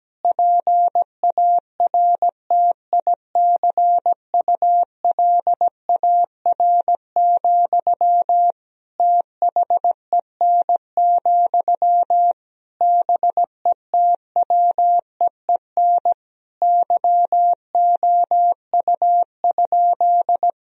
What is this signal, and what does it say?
Morse code, heard as a keyed tone, about 17 wpm, 700 Hz. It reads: PARTICULAR, THEN, BETWEEN YOU?
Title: Sentences from Top 1000 Words - Review Part 4 - 17wpm